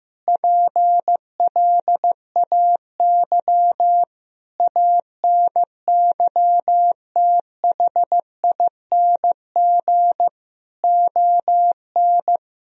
Morse code, heard as a keyed tone, about 16 wpm, 700 Hz. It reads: PLAY ANYTHING ON
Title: Sentences from Top 1000 Words - Review Part 4 - 15wpm